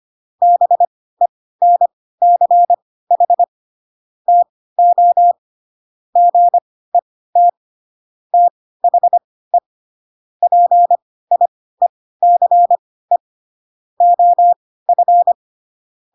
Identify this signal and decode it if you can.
Morse code, heard as a keyed tone, about 25 wpm, spaced as for 16 wpm, 700 Hz. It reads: BENCH TO GET THE PIECE OF
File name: Pinocchio / Pinocchio ch2 10 25 ik1hge